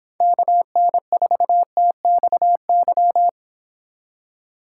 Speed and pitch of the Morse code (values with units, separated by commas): 26 wpm, 700 Hz